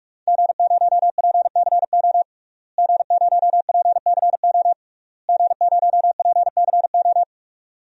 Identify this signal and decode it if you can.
Morse code, heard as a keyed tone, about 47 wpm, 700 Hz. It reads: G0PCO G0PCO G0PCO
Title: DXCC-100 Call Signs - 5 Characters - Repeated 3x - 45wpm